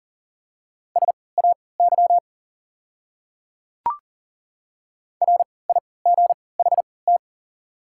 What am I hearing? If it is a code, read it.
Morse code, heard as a keyed tone, about 40 wpm, spaced as for 23 wpm, 690 Hz. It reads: SAY RIGHT
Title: 3x Character Spacing - Top 200 Words - 40wpm